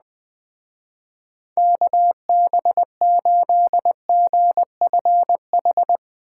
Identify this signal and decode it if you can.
Morse code, heard as a keyed tone, about 21 wpm, 700 Hz. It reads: KB8GFH